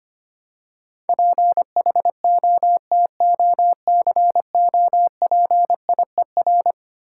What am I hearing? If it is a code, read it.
Morse code, heard as a keyed tone, about 25 wpm, 700 Hz. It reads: PHOTOCOPIER
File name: Easy Words - Speed-Racing with Hints - 25wpm